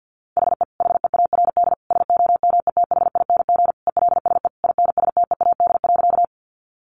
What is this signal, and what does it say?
Morse code, heard as a keyed tone, about 50 wpm, 700 Hz. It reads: HE HEARD SOMETHING ELSE INSTEAD.
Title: Sentences from Top 800 Words 50wpm